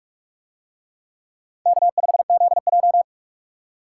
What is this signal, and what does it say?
Morse code, heard as a keyed tone, about 44 wpm, 700 Hz. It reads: K5ZJ